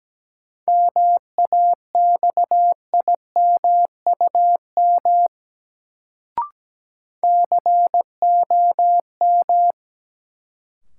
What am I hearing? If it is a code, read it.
Morse code, heard as a keyed tone, about 17 wpm, 700 Hz. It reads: MAXIMUM COM
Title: Sentences from Top 1300 Words - New Word Drill - 17wpm